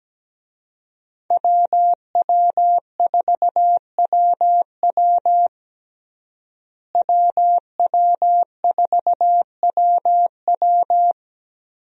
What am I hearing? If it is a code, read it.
Morse code, heard as a keyed tone, about 17 wpm, 700 Hz. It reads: WW4WW WW4WW
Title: USA Call Signs - 5 Characters - Repeated 2x - 17wpm